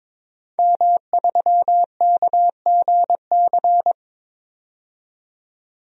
Morse code, heard as a keyed tone, about 22 wpm, 700 Hz. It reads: M3KGC
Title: DXCC-100 Call Signs - 5 Characters - 22wpm